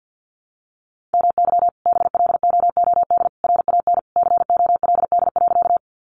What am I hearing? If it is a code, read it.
Morse code, heard as a keyed tone, about 50 wpm, 700 Hz. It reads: MY BLOOD RAN COLD.
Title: Sentences from Top 1100 Words 50wpm